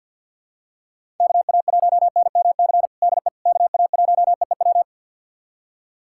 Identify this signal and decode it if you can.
Morse code, heard as a keyed tone, about 46 wpm, 700 Hz. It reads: KA1NMC DE KA1EEW
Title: USA Callsign DE Callsign 50wpm